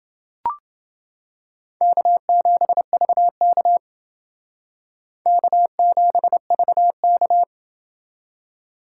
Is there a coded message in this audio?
Yes: Morse code from 1.8 to 7.4 s, heard as a keyed tone, about 29 wpm, 700 Hz. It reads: K7VK K7VK